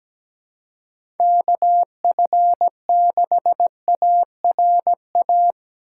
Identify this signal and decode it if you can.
Morse code, heard as a keyed tone, about 17 wpm, 700 Hz. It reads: KF6ARA
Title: USA Call Signs - 6 Characters - 4x Character Spacing - 17wpm